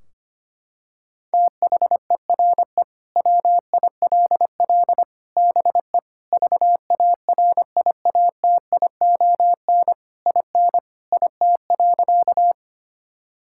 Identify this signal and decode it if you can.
Morse code, heard as a keyed tone, about 25 wpm, 700 Hz. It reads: THERE WILL BE VARIATION IN IT.
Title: Sentences from Top 1600 Words 25wpm